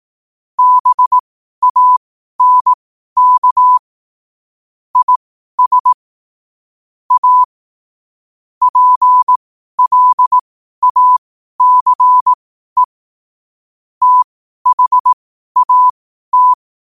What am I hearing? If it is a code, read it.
Morse code, heard as a keyed tone, about 18 wpm, spaced as for 12 wpm, 1000 Hz. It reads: BANK IS A PLACE THAT
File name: qotdfiles / 12wpm